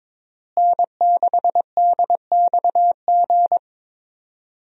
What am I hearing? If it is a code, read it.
Morse code, heard as a keyed tone, about 22 wpm, 700 Hz. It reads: N6DXG
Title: USA Call Signs - 5 Characters - Repeated 3x - 22wpm